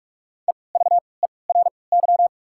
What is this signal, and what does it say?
Morse code, heard as a keyed tone, about 44 wpm, spaced as for 25 wpm, 690 Hz. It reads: EVERY
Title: 3x Character Spacing - Top 200 Words - 45wpm